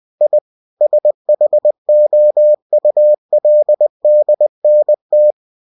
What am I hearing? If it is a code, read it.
Morse code, heard as a keyed tone, about 20 wpm, 600 Hz. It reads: I SHOULDNT
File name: cw-books / winnie-the-pooh / winnie-20wpm-page-0065